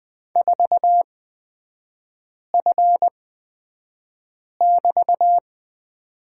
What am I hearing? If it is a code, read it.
Morse code, heard as a keyed tone, about 20 wpm, spaced as for 5 wpm, 700 Hz. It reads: 4F<BT>
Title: 236 Lesson 38e - Letter q - Three Characters 20wpm